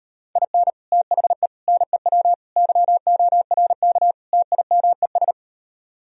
Morse code, heard as a keyed tone, about 39 wpm, 700 Hz. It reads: IN THE NEW YORK TIMES